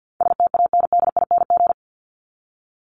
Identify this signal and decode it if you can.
Morse code, heard as a keyed tone, about 50 wpm, 700 Hz. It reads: STANDING
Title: ING Suffix - Top 100 Words with Base Words 50wpm